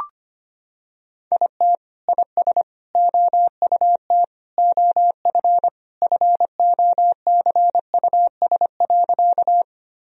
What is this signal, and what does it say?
Morse code, heard as a keyed tone, about 25 wpm, 700 Hz. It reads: IT IS OUT OF FOCUS.
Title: Sentences from Top 1000 Words - Review Part 4 - 25wpm